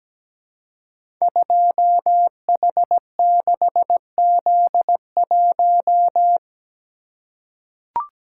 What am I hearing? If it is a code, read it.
Morse code, heard as a keyed tone, about 17 wpm, 700 Hz. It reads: 2H6Z1
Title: Code Groups - 5 Letter:Numbers - 17wpm